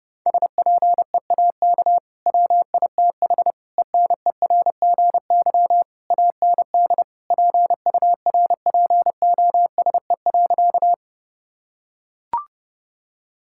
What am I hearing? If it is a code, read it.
Morse code, heard as a keyed tone, about 30 wpm, 700 Hz. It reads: SPEAK WITH ENERGY AND PURPOSE.
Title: Sentences from Top 1000 Words - Review Part 6 - 30wpm